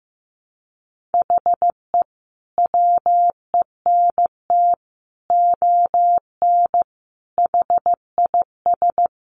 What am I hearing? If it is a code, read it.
Morse code, heard as a keyed tone, about 14 wpm, 700 Hz. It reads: HE WENT ON HIS
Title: Sentences from Top 200 Words 15wpm